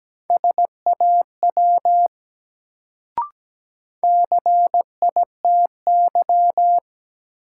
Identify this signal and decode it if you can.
Morse code, heard as a keyed tone, about 17 wpm, 700 Hz. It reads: SAW CITY
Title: Sentences from Top 300 Words - New Word Drill - 17wpm